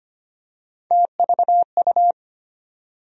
Morse code, heard as a keyed tone, about 25 wpm, 700 Hz. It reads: TVU
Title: USA Call Sign Postfixes - 3 Letters 25wpm